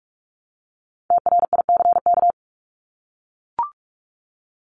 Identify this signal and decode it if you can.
Morse code, heard as a keyed tone, about 44 wpm, 700 Hz. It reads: TRICK
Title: Top 1251-1500 Words 45wpm